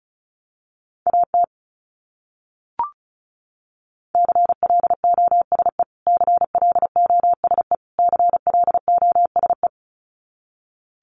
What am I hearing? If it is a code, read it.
Morse code, heard as a keyed tone, about 34 wpm, 700 Hz. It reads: AT CLOSE CLOSE CLOSE
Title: Top 500 Words Repeated 35wpm